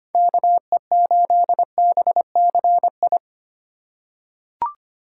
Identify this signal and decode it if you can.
Morse code, heard as a keyed tone, about 25 wpm, 700 Hz. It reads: KE8BCI
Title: USA Call Signs - 6 Characters - 2x Character Spacing - 25wpm